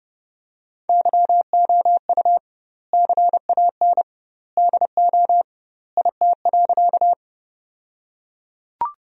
Morse code, heard as a keyed tone, about 32 wpm, 700 Hz. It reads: YOU CAN DO IT.
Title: 2x Word-Spacing - Sentences from Top 100 Words - 30wpm